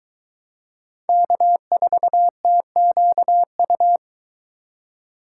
Morse code, heard as a keyed tone, about 22 wpm, 700 Hz. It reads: K4TQU